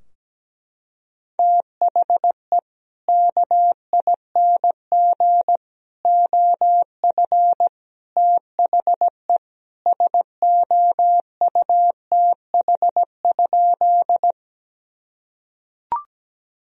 Morse code, heard as a keyed tone, about 17 wpm, 700 Hz. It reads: THE KING OF THE SOUTH?
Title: Sentences from Top 1000 Words - Review Part 2 - 17wpm